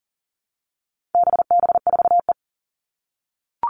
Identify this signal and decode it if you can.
Morse code, heard as a keyed tone, about 40 wpm, 700 Hz. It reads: BB4E
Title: All Cut Numbers - 4-Digit 40wpm